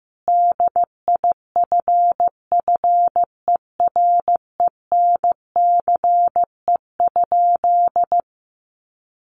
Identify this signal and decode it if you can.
Morse code, heard as a keyed tone, about 15 wpm, 700 Hz. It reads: DIFFERENCE?
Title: Sentences from Top 800 Words 15wpm